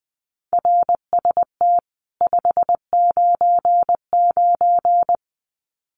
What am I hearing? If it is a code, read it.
Morse code, heard as a keyed tone, about 20 wpm, 700 Hz. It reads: RST 599
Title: Common QSO Phrases 20wpm